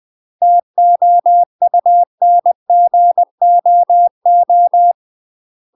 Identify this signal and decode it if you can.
Morse code, heard as a keyed tone, about 20 wpm, 700 Hz. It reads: TOUNGOO
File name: UK MASTER QSOs TRAINING / 20WPM / UK MASTER QSOs 0072